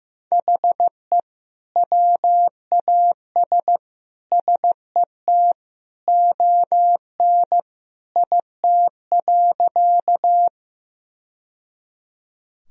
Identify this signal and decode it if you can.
Morse code, heard as a keyed tone, about 15 wpm, 700 Hz. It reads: HE WAS SET ON IT.
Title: Sentences from Top 1000 Words - Review Part 1 - 15wpm